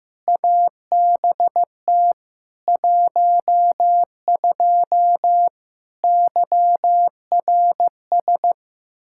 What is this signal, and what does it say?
Morse code, heard as a keyed tone, about 15 wpm, 700 Hz. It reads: ABT 12 YRS
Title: Practice QSO - Exchange 3 - More Info - 15wpm